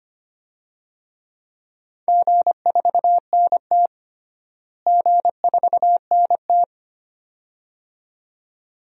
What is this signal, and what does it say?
Morse code, heard as a keyed tone, about 25 wpm, 700 Hz. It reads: G4NT G4NT
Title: DXCC-100 Call Signs - 4 Characters - Repeated 2x - 25wpm